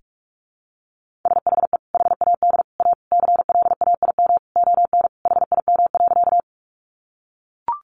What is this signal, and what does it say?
Morse code, heard as a keyed tone, about 44 wpm, 700 Hz. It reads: SHE HAD A CLAIM ON HIM.
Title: Sentences from Top 1500 Words 45wpm